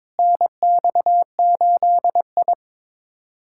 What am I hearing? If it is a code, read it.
Morse code, heard as a keyed tone, about 22 wpm, 700 Hz. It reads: NX8I
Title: USA Callsign DE Callsign - 4 Characters - 22wpm